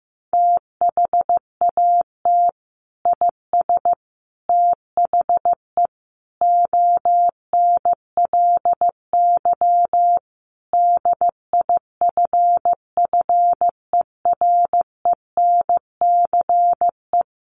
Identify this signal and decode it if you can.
Morse code, heard as a keyed tone, about 14 wpm, 700 Hz. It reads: THAT IS THE ONLY DIFFERENCE
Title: Sentences from Top 800 Words 15wpm